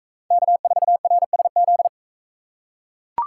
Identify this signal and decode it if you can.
Morse code, heard as a keyed tone, about 43 wpm, 700 Hz. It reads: K4RSZ